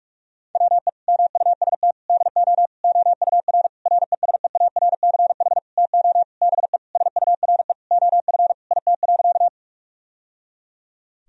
Morse code, heard as a keyed tone, about 46 wpm, 700 Hz. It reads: WE MUST DO OUR RESEARCH TO BE SURE OF IT.
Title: Sentences from Top 1000 Words - Review Part 4 - 45wpm